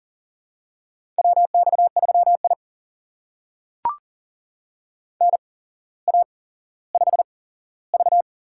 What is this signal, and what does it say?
Morse code, heard as a keyed tone, about 42 wpm, spaced as for 17 wpm, 700 Hz. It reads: WX3I NA5V
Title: USA Call Signs - 4 Characters - 8x Character Spacing - 40wpm